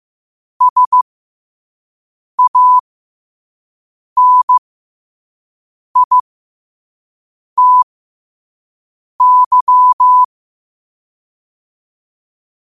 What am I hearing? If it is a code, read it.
Morse code, heard as a keyed tone, about 15 wpm, spaced as for 5 wpm, 1000 Hz. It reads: SANITY
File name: qotdfiles / 5wpm